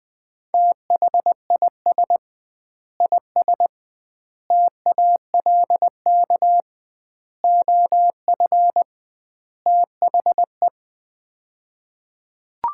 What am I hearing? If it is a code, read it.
Morse code, heard as a keyed tone, about 20 wpm, 700 Hz. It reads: THIS IS TALK OF THE
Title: 2x Word-Spacing - Sets of 5 Words - 20wpm